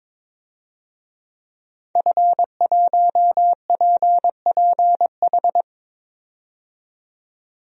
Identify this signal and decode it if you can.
Morse code, heard as a keyed tone, about 22 wpm, 700 Hz. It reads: F1PPH
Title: DXCC-100 Call Signs - 5 Characters - 22wpm